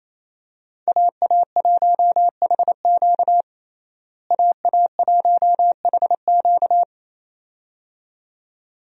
Morse code, heard as a keyed tone, about 28 wpm, 700 Hz. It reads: AA1HQ AA1HQ
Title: USA Call Signs - 5 Characters - Repeated 2x - 28wpm